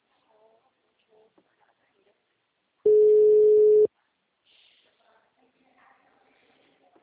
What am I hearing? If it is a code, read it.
Ringback tone (Europe, 425 Hz), with speech.